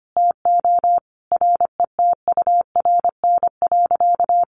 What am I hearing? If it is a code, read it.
Morse code, heard as a keyed tone, about 25 wpm, 700 Hz. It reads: TO RETURN.
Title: Sentences from Top 800 Words 25wpm